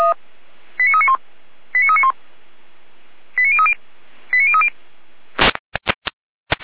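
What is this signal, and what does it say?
Five-tone selective call (ZVEI), heard as quick runs of five tones, tones of about 70 ms. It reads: 89391 89391 89030 89030
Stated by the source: ZVEI five-tone selective call received on an SDR.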